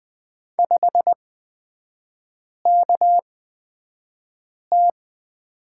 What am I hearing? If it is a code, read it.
Morse code, heard as a keyed tone, about 20 wpm, spaced as for 5 wpm, 700 Hz. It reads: 5KT
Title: 224 Lesson 36e - BT - Three Characters 20wpm